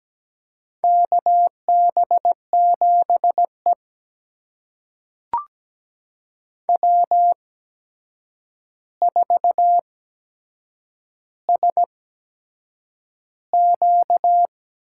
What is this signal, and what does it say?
Morse code, heard as a keyed tone, about 17 wpm, spaced as for 7 wpm, 700 Hz. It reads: KB7E W4SQ
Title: USA Call Signs - 4 Characters - 8x Character Spacing - 17wpm